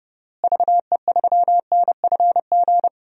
Morse code, heard as a keyed tone, about 32 wpm, 700 Hz. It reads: VE3NFG
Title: DXCC-100 Call Signs - 6 Characters - Repeated 3x - 30wpm